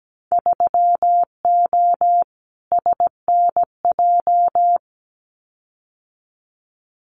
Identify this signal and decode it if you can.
Morse code, heard as a keyed tone, about 17 wpm, 700 Hz. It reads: 3O SNJ
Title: Winter Field Day Exchanges Encore 17wpm